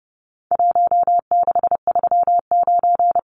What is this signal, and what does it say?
Morse code, heard as a keyed tone, about 30 wpm, 700 Hz. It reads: 1639
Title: Numbers - Contest Style - 4-Digit 30wpm